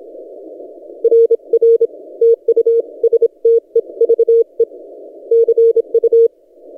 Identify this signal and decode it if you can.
Morse code, heard as a keyed tone, about 27 wpm, 460 Hz. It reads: RR TU STEVE CU
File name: posts / t32rc-30cw